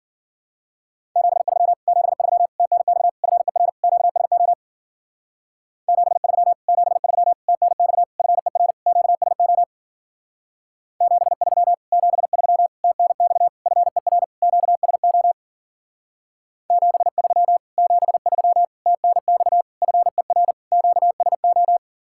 Morse code, heard as a keyed tone, about 52 wpm, 700 Hz. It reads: 73 73 TNX FER QSO 73 73 TNX FER QSO 73 73 TNX FER QSO 73 73 TNX FER QSO